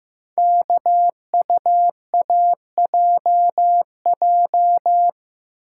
Morse code, heard as a keyed tone, about 15 wpm, 700 Hz. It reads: KUAJJ
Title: Code Groups - 5 Letters - 15wpm